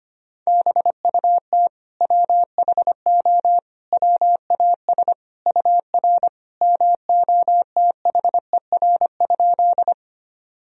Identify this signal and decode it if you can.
Morse code, heard as a keyed tone, about 25 wpm, 700 Hz. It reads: BUT WHO WAS UR MOTHER?
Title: Sentences from Top 200 Words and Abbreviations 25wpm